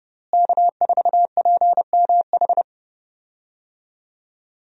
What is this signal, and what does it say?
Morse code, heard as a keyed tone, about 30 wpm, 700 Hz. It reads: K4PMH
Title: USA Call Signs - 5 Characters - Repeated 3x - 30wpm